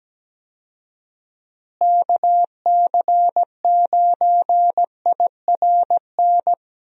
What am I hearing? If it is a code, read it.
Morse code, heard as a keyed tone, about 17 wpm, 700 Hz. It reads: KC9IRN
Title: USA Call Signs - 6 Characters - Repeated 3x - 17wpm